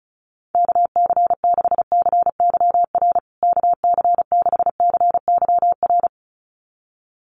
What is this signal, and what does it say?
Morse code, heard as a keyed tone, about 34 wpm, 700 Hz. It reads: KC6CYR KC6CYR
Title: USA Call Signs Repeated 2x 35wpm